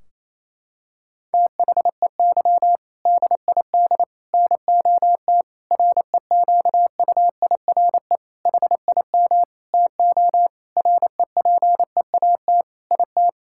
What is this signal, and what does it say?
Morse code, heard as a keyed tone, about 28 wpm, 700 Hz. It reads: THEY DID NOT REQUIRE HIM TO REPEAT IT
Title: Sentences from Top 1600 Words 28wpm